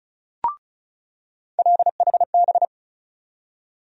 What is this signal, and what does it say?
Morse code, heard as a keyed tone, about 34 wpm, 700 Hz. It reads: LHB
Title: USA Call Sign Postfixes - 3 Letters 35wpm